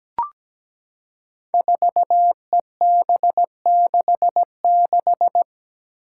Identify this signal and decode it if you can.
Morse code, heard as a keyed tone, about 17 wpm, 700 Hz. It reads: 4EB66
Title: Five Challenging Letter-Number Combinations 17wpm